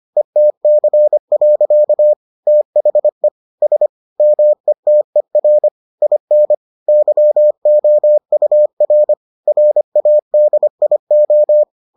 Morse code, heard as a keyed tone, about 25 wpm, 600 Hz. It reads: ETC. THE S METER IN YOUR RADIO